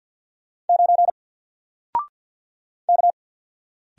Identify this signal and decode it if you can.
Morse code, heard as a keyed tone, about 50 wpm, 700 Hz. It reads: 9 K
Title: Single Character (Letter:Number:Punctuation:Prosign) 50wpm